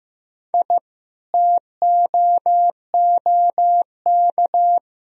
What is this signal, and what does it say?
Morse code, heard as a keyed tone, about 15 wpm, 700 Hz. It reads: I TOOK
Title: Sentences from Top 1000 Words - Review Part 2 - 15wpm